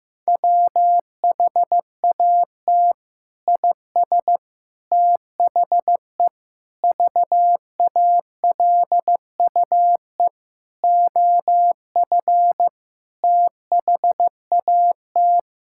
Morse code, heard as a keyed tone, about 16 wpm, 700 Hz. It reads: WHAT IS THE VALUE OF THAT
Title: Sentences from Top 1000 Words - Review Part 6 - 15wpm